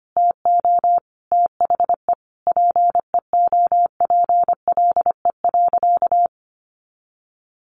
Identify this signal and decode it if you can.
Morse code, heard as a keyed tone, about 25 wpm, 700 Hz. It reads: TO THE PEOPLE.
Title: Sentences from Top 400 Words 25wpm